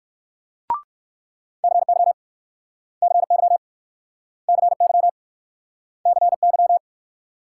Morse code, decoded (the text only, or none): CY CY CY CY